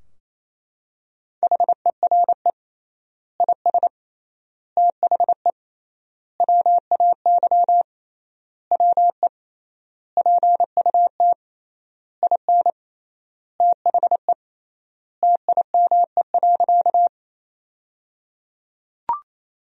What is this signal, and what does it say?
Morse code, heard as a keyed tone, about 28 wpm, 700 Hz. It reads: HERE IS THE WAY WE PUT IN THE TIME.
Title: 3x Word-Spacing - Sentences from Top 200 Words - 28wpm